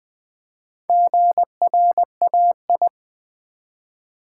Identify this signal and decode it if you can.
Morse code, heard as a keyed tone, about 20 wpm, 700 Hz. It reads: GRAI
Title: Build A Word - 1 Letter At A Time - 5-Letter Words - 20wpm